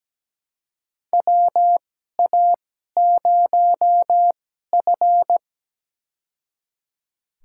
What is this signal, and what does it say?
Morse code, heard as a keyed tone, about 17 wpm, spaced as for 12 wpm, 700 Hz. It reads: WA0F
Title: USA Call Signs - 4 Characters - 2x Character Spacing - 17wpm